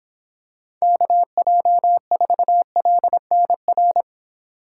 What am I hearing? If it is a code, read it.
Morse code, heard as a keyed tone, about 22 wpm, 700 Hz. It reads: KJ4LNR